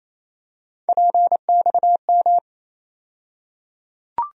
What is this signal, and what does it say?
Morse code, heard as a keyed tone, about 28 wpm, 700 Hz. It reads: PXM
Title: USA Call Sign Postfixes - 3 Letters 28wpm